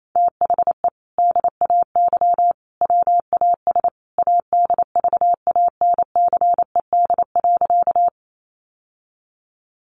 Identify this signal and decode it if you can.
Morse code, heard as a keyed tone, about 28 wpm, 700 Hz. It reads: THE DAY WAS ADVANCED.
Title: Sentences from Top 1200 Words 28wpm